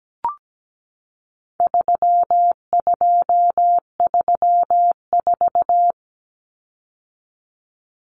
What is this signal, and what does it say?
Morse code, heard as a keyed tone, about 17 wpm, 700 Hz. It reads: 3234
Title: Common Cut Numbers - 4-Digit 17wpm